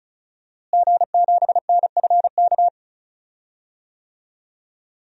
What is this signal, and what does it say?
Morse code, heard as a keyed tone, about 36 wpm, 700 Hz. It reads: G7NFK
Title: DXCC-100 Call Signs - 5 Characters - Repeated 3x - 35wpm